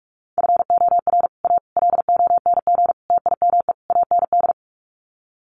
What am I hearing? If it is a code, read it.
Morse code, heard as a keyed tone, about 45 wpm, 700 Hz. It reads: FOR A LONG TIME AND
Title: Sets of 5 Words Encore2 45wpm